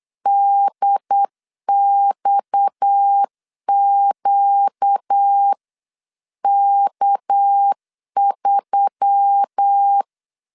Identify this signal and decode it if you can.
Morse code, heard as a keyed tone, about 8 wpm, 800 Hz. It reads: DXQK3
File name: 8wpm